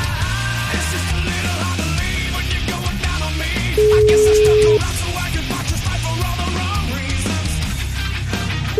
Ringback tone (Europe, 425 Hz), with noise.